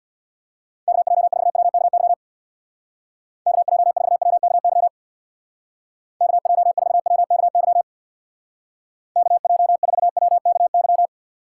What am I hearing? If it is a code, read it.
Morse code, heard as a keyed tone, about 60 wpm, 700 Hz. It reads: KJ4WKY KJ4WKY KJ4WKY KJ4WKY
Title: Speed Racing - USA 2x3 Call Signs 50wpm